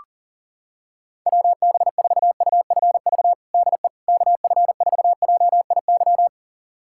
Morse code, heard as a keyed tone, about 40 wpm, 700 Hz. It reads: WB4UFV DE KF4JIY
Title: USA Callsign DE Callsign - 6 Characters - 40wpm